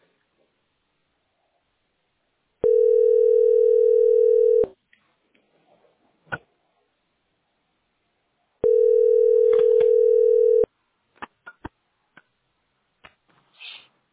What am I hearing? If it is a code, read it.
Ringback tone (North America), 440 + 480 Hz.